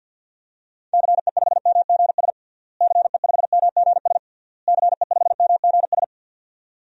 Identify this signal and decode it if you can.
Morse code, heard as a keyed tone, about 52 wpm, 700 Hz. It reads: CE5MGS CE5MGS CE5MGS
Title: DXCC-100 Call Signs - 6 Characters - Repeated 3x - 50wpm